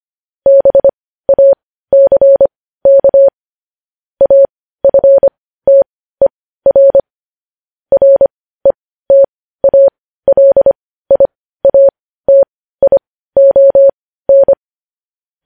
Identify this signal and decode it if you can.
Morse code, heard as a keyed tone, about 25 wpm, spaced as for 15 wpm, 550 Hz. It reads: BACK AFTER RETALIATION
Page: news headlines in morse code at 15 wpm